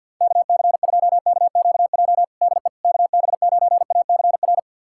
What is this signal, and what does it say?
Morse code, heard as a keyed tone, about 51 wpm, 690 Hz. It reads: KC2KQJ DE KB9ACR
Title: USA Callsign DE Callsign - 6 Characters - 50wpm